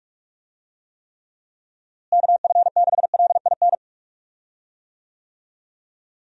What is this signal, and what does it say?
Morse code, heard as a keyed tone, about 46 wpm, 700 Hz. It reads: KF6LIN
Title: USA Call Signs - 6 Characters - Repeated 2x - 45wpm